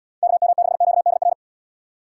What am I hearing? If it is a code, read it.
Morse code, heard as a keyed tone, about 78 wpm, 700 Hz. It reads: KM6QDR